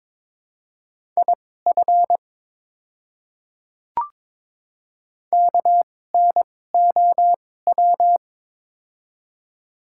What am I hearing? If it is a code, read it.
Morse code, heard as a keyed tone, about 22 wpm, spaced as for 16 wpm, 700 Hz. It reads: IF KNOW
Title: 2x Character Spacing - Top 100 Words - 22wpm